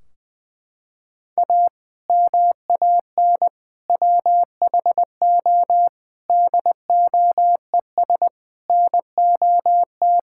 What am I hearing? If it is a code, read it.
Morse code, heard as a keyed tone, about 20 wpm, 700 Hz. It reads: A MAN WHO DOES NOT
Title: Sentences from Top 1600 Words 20wpm